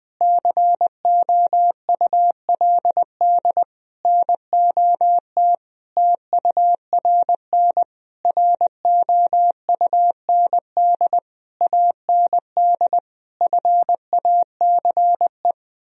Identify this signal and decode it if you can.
Morse code, heard as a keyed tone, about 20 wpm, 700 Hz. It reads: COULD NOT TURN ROUND AND FACE
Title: Sentences from Top 1000 Words - Review Part 2 - 20wpm